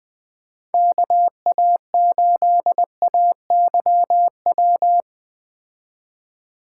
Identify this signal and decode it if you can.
Morse code, heard as a keyed tone, about 20 wpm, 700 Hz. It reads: KA8AYW
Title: Speed Racing - USA 2x3 Call Signs 17wpm